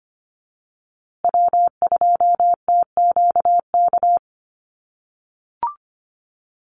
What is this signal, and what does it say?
Morse code, heard as a keyed tone, about 25 wpm, 700 Hz. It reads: W2TQK
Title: USA Call Signs Repeated 3x 25wpm